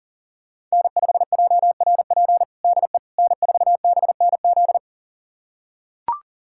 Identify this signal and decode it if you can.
Morse code, heard as a keyed tone, about 40 wpm, 700 Hz. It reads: N5JRP DE N4BNZ
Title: USA Callsign DE Callsign - 5 Characters - 40wpm